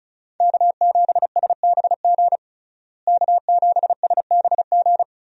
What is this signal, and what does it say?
Morse code, heard as a keyed tone, about 36 wpm, 700 Hz. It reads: K7SBG K7SBG
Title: USA Call Signs - 5 Characters - Repeated 2x - 35wpm